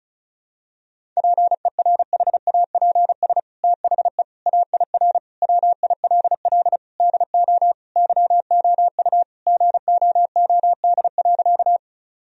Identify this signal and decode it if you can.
Morse code, heard as a keyed tone, about 35 wpm, 700 Hz. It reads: PERHAPS THE AIR WILL DO YOU GOOD.
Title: Sentences from Top 1000 Words - Review Part 1 - 35wpm